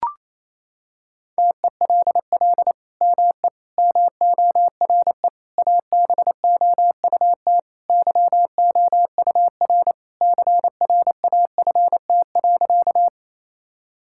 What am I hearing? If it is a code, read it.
Morse code, heard as a keyed tone, about 28 wpm, 700 Hz. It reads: TELL ME MORE ABOUT YOUR CRAFT.
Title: Sentences from Top 1000 Words - Review Part 6 - 28wpm